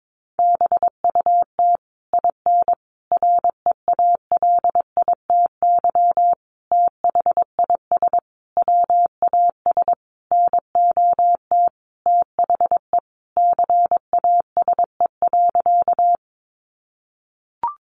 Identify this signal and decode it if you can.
Morse code, heard as a keyed tone, about 22 wpm, 700 Hz. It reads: BUT IN REALITY THIS WAS NOT THE CASE.
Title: Sentences from Top 1200 Words 22wpm